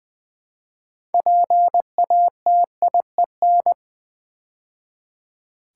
Morse code, heard as a keyed tone, about 20 wpm, 700 Hz. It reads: PATIEN
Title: Build A Word - 2 Letters At A Time - 8-Letter Words - 20wpm